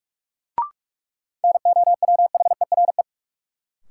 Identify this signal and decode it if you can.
Morse code, heard as a keyed tone, about 46 wpm, 700 Hz. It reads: NOWHERE
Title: Sets of 5 Words Encore - New Word Drill - 45wpm